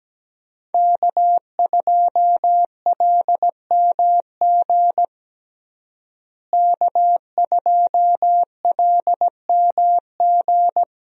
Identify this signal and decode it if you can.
Morse code, heard as a keyed tone, about 17 wpm, 700 Hz. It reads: K2LMG K2LMG
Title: USA Call Signs - 5 Characters - Repeated 3x - 17wpm